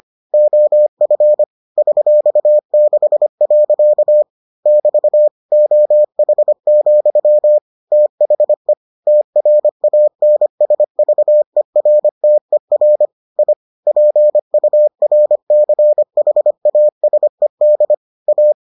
Morse code, heard as a keyed tone, about 25 wpm, 600 Hz. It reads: OF $6. <BT> OH, THE TRANSVERTER I PURCHASED A